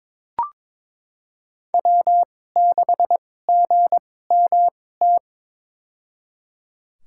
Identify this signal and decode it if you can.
Morse code, heard as a keyed tone, about 22 wpm, spaced as for 16 wpm, 700 Hz. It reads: W6GMT
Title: USA Call Signs - 2x Character Spacing - 22wpm